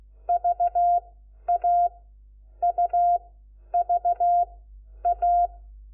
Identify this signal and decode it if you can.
Morse code, heard as a keyed tone, about 16 wpm, spaced as for 10 wpm, 700 Hz. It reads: VAUVA
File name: Lecciones / CLASE Nº4